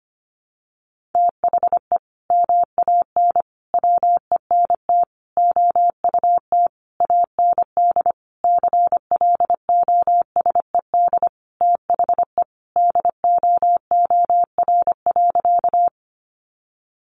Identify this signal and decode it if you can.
Morse code, heard as a keyed tone, about 25 wpm, 700 Hz. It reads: THE MAN WENT OUT AND CLOSED THE DOOR.
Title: Sentences from Top 1400 Words 25wpm